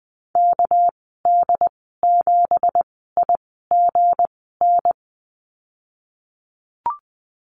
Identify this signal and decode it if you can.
Morse code, heard as a keyed tone, about 20 wpm, spaced as for 15 wpm, 700 Hz. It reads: KD7IGN
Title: 207 Lesson 33g - Letter j - Call Signs 20wpm